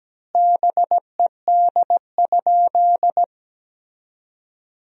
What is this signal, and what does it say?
Morse code, heard as a keyed tone, about 17 wpm, 700 Hz. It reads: BED?
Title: 3x Word-Spacing - Sentences from Top 500 Words - 17wpm